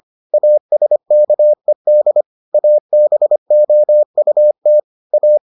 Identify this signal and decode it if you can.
Morse code, heard as a keyed tone, about 25 wpm, 600 Hz. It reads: ASKED ABOUT A